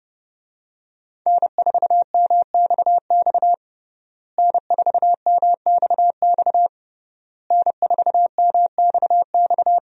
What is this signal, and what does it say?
Morse code, heard as a keyed tone, about 28 wpm, 700 Hz. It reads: N4MXX N4MXX N4MXX
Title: USA Call Signs - 5 Characters - Repeated 3x - 30wpm